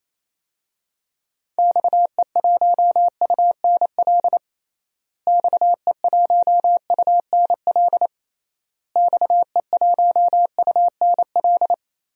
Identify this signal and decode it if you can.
Morse code, heard as a keyed tone, about 28 wpm, 700 Hz. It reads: XE1UNL XE1UNL XE1UNL
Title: DXCC-100 Call Signs - 6 Characters - Repeated 3x - 28wpm